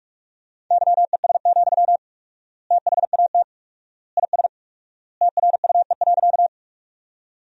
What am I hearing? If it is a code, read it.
Morse code, heard as a keyed tone, about 46 wpm, 690 Hz. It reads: YES, THAT IS TRUE.
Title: 4x Word-Spacing - Sentences from Top 500 Words - 45wpm